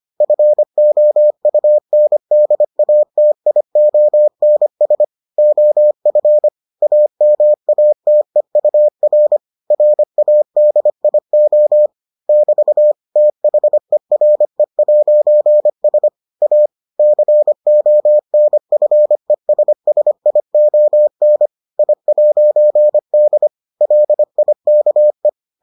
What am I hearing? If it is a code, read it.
Morse code, heard as a keyed tone, about 25 wpm, 600 Hz. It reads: FOUNDATIONS OF AMATEUR RADIO <BT> THERE'S A CONFESSION I'D LIKE